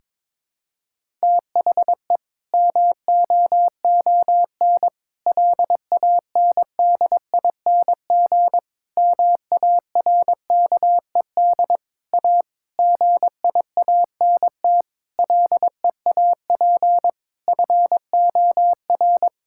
Morse code, heard as a keyed tone, about 22 wpm, 700 Hz. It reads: THE MOON LANDING MARKED A GIANT LEAP FOR
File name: Difficult Sentences with Hints - 22wpm